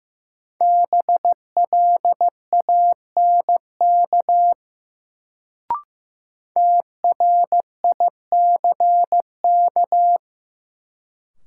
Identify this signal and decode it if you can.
Morse code, heard as a keyed tone, about 15 wpm, 700 Hz. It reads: BLANK TRICK
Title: Sentences from Top 1500 Words - New Word Drill - 15wpm